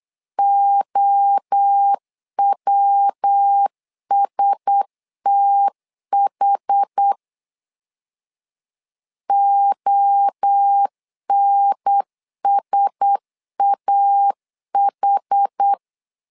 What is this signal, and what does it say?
Morse code, heard as a keyed tone, about 8 wpm, 800 Hz. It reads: OWSTH ONSAH
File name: LEZIONI / 4 Q1S 8wpm 2710